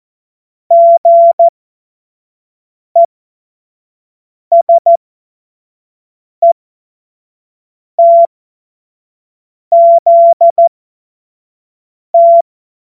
Morse code, heard as a keyed tone, about 14 wpm, spaced as for 5 wpm, 680 Hz. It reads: GESETZT
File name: Morsekurs / Witze / Witz 22. Februar 2022 25 F70